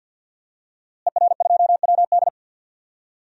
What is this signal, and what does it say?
Morse code, heard as a keyed tone, about 49 wpm, 700 Hz. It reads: ER2WD
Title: DXCC-100 Call Signs - 5 Characters - Repeated 3x - 50wpm